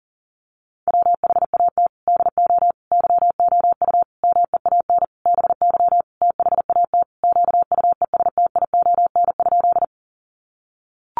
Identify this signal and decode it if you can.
Morse code, heard as a keyed tone, about 40 wpm, 700 Hz. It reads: WHAT DO YOU MEAN BY THAT QUESTION?
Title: Sentences from Top 400 Words 40wpm